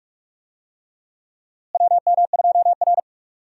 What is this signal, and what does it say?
Morse code, heard as a keyed tone, about 45 wpm, 700 Hz. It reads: WM2R
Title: USA Call Signs - 4 Characters - Repeated 3x - 45wpm